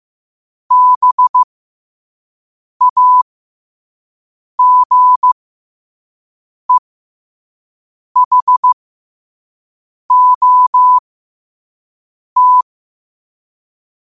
Morse code, heard as a keyed tone, about 15 wpm, spaced as for 5 wpm, 1000 Hz. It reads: BAGEHOT
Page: qotd-at-5-wpm